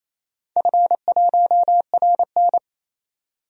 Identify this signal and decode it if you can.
Morse code, heard as a keyed tone, about 28 wpm, 700 Hz. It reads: F1RN
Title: DXCC-100 Call Signs - 4 Characters - 28wpm